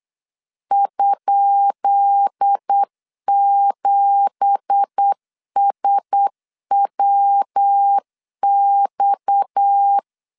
Morse code, heard as a keyed tone, about 8 wpm, 800 Hz. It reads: ?7SWX
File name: LEZIONI / 10 SIGNS / 10D COMA 8wpm 3749